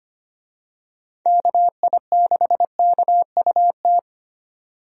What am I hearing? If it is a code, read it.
Morse code, heard as a keyed tone, about 25 wpm, 700 Hz. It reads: KI6KUT
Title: USA Call Signs - 6 Characters - 25wpm